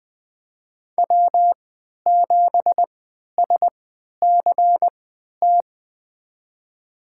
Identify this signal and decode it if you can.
Morse code, heard as a keyed tone, about 20 wpm, spaced as for 11 wpm, 700 Hz. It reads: W7SCT
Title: USA Call Signs - 3x Character Spacing - 20wpm